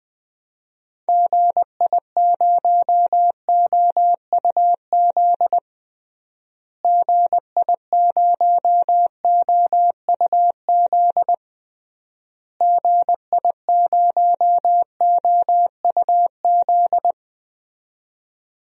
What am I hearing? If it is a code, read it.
Morse code, heard as a keyed tone, about 19 wpm, 700 Hz. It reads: GI0OUZ GI0OUZ GI0OUZ
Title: DXCC-100 Call Signs - 6 Characters - Repeated 3x - 20wpm